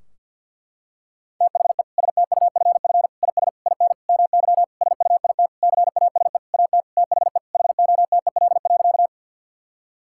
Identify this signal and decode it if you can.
Morse code, heard as a keyed tone, about 50 wpm, 700 Hz. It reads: THE STUFF IS IN MY SUIT CASE AT THE HOTEL.
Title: Sentences from Top 1600 Words 50wpm